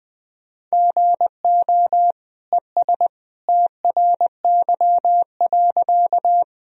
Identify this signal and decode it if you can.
Morse code, heard as a keyed tone, about 21 wpm, 700 Hz. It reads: GO ES TRY.
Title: Sentences from Top 200 Words and Abbreviations 20wpm